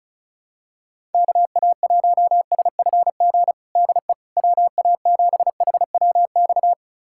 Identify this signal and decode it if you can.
Morse code, heard as a keyed tone, about 36 wpm, 700 Hz. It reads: KA1SFG DE WA7HWX
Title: USA Callsign DE Callsign 35wpm